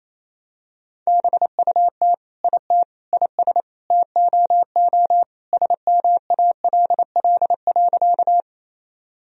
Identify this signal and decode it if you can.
Morse code, heard as a keyed tone, about 28 wpm, 700 Hz. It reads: BUT IT IS TOO SMALL.
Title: Sentences from Top 1000 Words - Review Part 1 - 28wpm